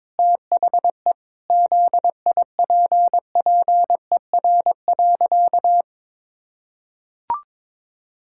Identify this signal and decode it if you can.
Morse code, heard as a keyed tone, about 22 wpm, 700 Hz. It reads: THE ZIPPER.
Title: Pangram Challenge v2 - 22wpm